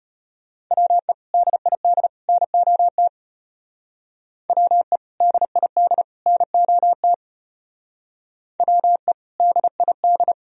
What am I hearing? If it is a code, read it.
Morse code, heard as a keyed tone, about 33 wpm, 700 Hz. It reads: WE DID NOT WE DID NOT WE DID